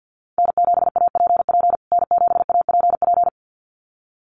Morse code, heard as a keyed tone, about 48 wpm, 700 Hz. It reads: N7APP N7APP
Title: USA Call Signs Repeated 2x 50wpm